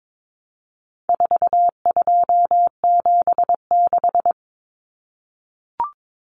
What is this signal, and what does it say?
Morse code, heard as a keyed tone, about 22 wpm, 700 Hz. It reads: 4276
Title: Common Cut Numbers - 4-Digit 22wpm